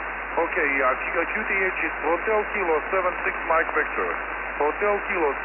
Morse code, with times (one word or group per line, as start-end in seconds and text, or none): none